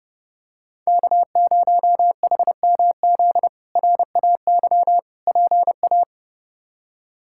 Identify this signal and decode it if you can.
Morse code, heard as a keyed tone, about 32 wpm, 700 Hz. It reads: K0HMZ RAY PA
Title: K1USN SST QSO Practice - 30wpm